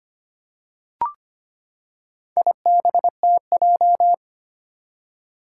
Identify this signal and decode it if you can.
Morse code, heard as a keyed tone, about 25 wpm, 700 Hz. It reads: IBTJ
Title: Four Challenging Letter-Number Combinations 25wpm